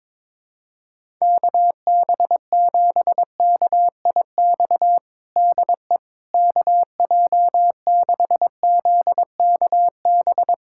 Morse code, heard as a keyed tone, about 22 wpm, 700 Hz. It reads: KB7KIX DE KJ6ZKB
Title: USA Callsign DE Callsign 22wpm